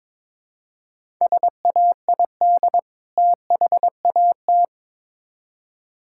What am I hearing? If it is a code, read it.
Morse code, heard as a keyed tone, about 22 wpm, 700 Hz. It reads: SAID THAT
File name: Build A Phrase - 2-Word Phrases - 22wpm